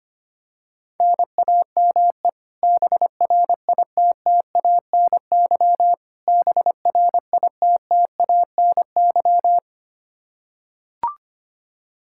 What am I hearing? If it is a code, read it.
Morse code, heard as a keyed tone, about 25 wpm, 700 Hz. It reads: NAME BRITTANY BRITTANY
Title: Practice QSO - Exchange 1 - Beginning - 25wpm